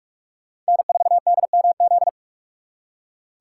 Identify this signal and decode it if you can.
Morse code, heard as a keyed tone, about 57 wpm, 700 Hz. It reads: N4DMZ